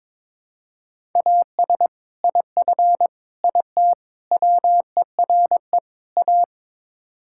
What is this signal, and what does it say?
Morse code, heard as a keyed tone, about 22 wpm, 700 Hz. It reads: AS IF IT WERE A
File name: practice / 22wpm / Build A Phrase - 5-Word Phrases - 22wpm